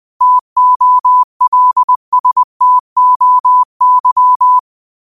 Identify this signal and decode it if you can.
Morse code, heard as a keyed tone, about 20 wpm, 1000 Hz. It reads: TOLSTOY
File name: qotdfiles / 20wpm